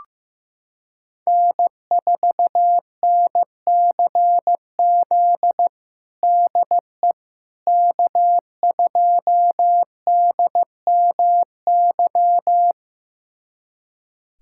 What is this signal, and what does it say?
Morse code, heard as a keyed tone, about 15 wpm, 700 Hz. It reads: N4NCZ DE K2DMY
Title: USA Callsign DE Callsign - 5 Characters - 15wpm